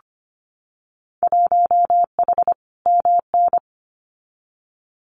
Morse code, heard as a keyed tone, about 25 wpm, 700 Hz. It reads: 1H MN
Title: Winter Field Day Exchanges 25wpm